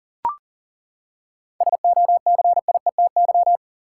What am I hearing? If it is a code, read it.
Morse code, heard as a keyed tone, about 40 wpm, 700 Hz. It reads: SOCIETY
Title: Seven Letter Words - 40wpm